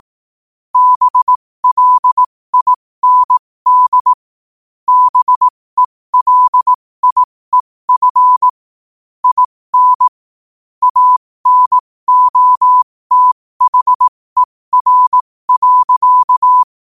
Morse code, heard as a keyed tone, about 18 wpm, spaced as for 15 wpm, 1000 Hz. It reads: BLIND BELIEF IN ANOTHER.